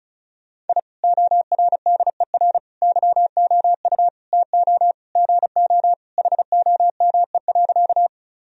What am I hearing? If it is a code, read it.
Morse code, heard as a keyed tone, about 36 wpm, 700 Hz. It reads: I ORDER YOU TO GO HOME.
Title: Sentences from Top 1000 Words - Review Part 2 - 35wpm